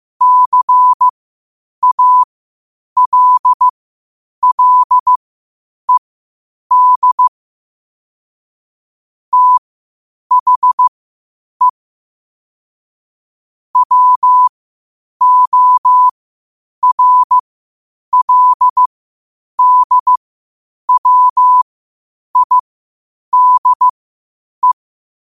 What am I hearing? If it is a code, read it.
Morse code, heard as a keyed tone, about 15 wpm, spaced as for 8 wpm, 1000 Hz. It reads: CALLED THE WORLDWIDE
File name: qotdfiles / 8wpm